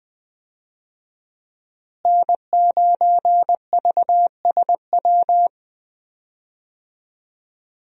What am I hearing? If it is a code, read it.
Morse code, heard as a keyed tone, about 20 wpm, 700 Hz. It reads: N9VSW
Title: USA Call Signs - 5 Characters - Repeated 2x - 20wpm